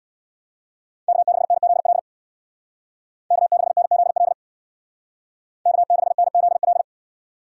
Morse code, heard as a keyed tone, about 64 wpm, 700 Hz. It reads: K6NZL K6NZL K6NZL